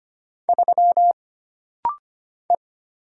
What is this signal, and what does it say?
Morse code, heard as a keyed tone, about 25 wpm, 700 Hz. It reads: 3 E
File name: Single Character (Letter:Number:Punctuation:Prosign) - Rapid-Fire - 25wpm